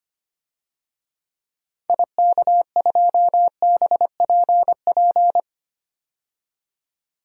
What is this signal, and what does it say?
Morse code, heard as a keyed tone, about 25 wpm, 700 Hz. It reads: IK2BPP
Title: DXCC-100 Call Signs - 6 Characters - 25wpm